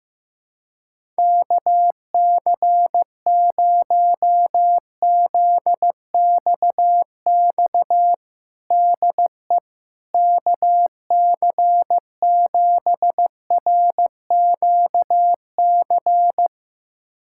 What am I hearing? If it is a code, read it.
Morse code, heard as a keyed tone, about 16 wpm, 700 Hz. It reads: KC0ZXX DE KC7RQC
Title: USA Callsign DE Callsign 15wpm